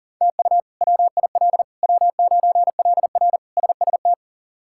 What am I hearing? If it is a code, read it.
Morse code, heard as a keyed tone, about 40 wpm, 700 Hz. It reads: TU WIL W9LR SST
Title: K1USN SST QSO Practice - 40wpm